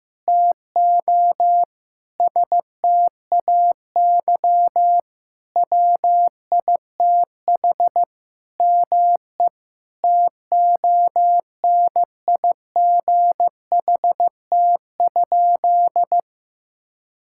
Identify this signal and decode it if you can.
Morse code, heard as a keyed tone, about 15 wpm, 700 Hz. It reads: TO STAY WITH ME TONIGHT?
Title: Sentences from Top 1600 Words 15wpm